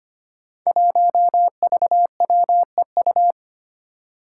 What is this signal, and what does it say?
Morse code, heard as a keyed tone, about 25 wpm, 700 Hz. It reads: 1VWEU
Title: Five Challenging Letter-Number Combinations 25wpm